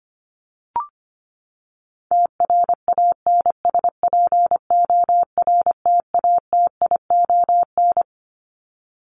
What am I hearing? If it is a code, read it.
Morse code, heard as a keyed tone, about 25 wpm, 700 Hz. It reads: TRANSPORTATION
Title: Top 1501-1750 Words 25wpm